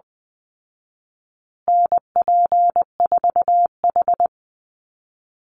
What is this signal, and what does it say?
Morse code, heard as a keyed tone, about 20 wpm, 700 Hz. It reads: NP4H
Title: Call Signs Repeated 3x - 20wpm